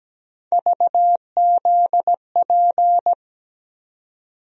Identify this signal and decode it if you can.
Morse code, heard as a keyed tone, about 17 wpm, 700 Hz. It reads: VZP
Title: USA Call Sign Postfixes - 3 Letters 17wpm